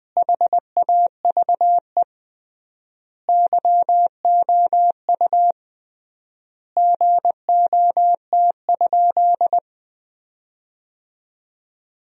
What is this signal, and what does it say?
Morse code, heard as a keyed tone, about 21 wpm, 700 Hz. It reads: HAVE YOU GOT?
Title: 3x Word-Spacing - Sentences from Top 300 Words - 20wpm